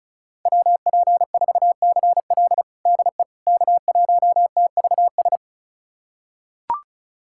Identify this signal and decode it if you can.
Morse code, heard as a keyed tone, about 34 wpm, 700 Hz. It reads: WP4CL DE K1TVS
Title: USA Callsign DE Callsign - 5 Characters - 35wpm